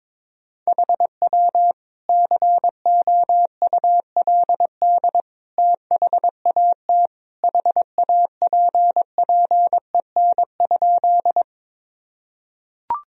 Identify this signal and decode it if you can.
Morse code, heard as a keyed tone, about 22 wpm, 700 Hz. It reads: HW COULD THAT HAPPEN?
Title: Sentences from Top 400 Words and Abbreviations 22wpm